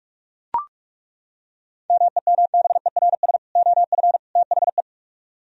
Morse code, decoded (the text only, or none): MEMBERS OF THE